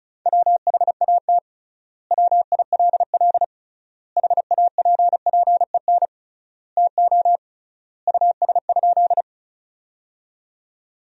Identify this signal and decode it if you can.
Morse code, heard as a keyed tone, about 34 wpm, 700 Hz. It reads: WHAT WILL HAPPEN TO US?
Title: 3x Word-Spacing - Sentences from Top 400 Words - 35wpm